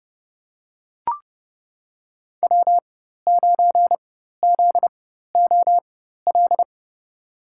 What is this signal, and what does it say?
Morse code, heard as a keyed tone, about 31 wpm, spaced as for 14 wpm, 700 Hz. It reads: W9ZOL
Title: USA Call Signs - 5 Characters - 4x Character Spacing - 30wpm